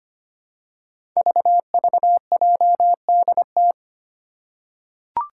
Morse code, heard as a keyed tone, about 25 wpm, 700 Hz. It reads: VVJDT
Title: Five Challenging Letter-Number Combinations 25wpm